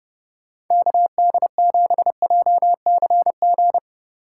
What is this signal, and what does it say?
Morse code, heard as a keyed tone, about 29 wpm, 700 Hz. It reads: KD7JCG
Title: USA Call Signs - 2x Character Spacing - 30wpm